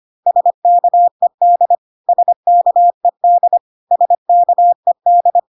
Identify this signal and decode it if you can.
Morse code, heard as a keyed tone, about 25 wpm, 700 Hz. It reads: SKED SKED SKED
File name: cw / CW common abbreviations alphabetical repeated 3 times 25 25wpm ik1hge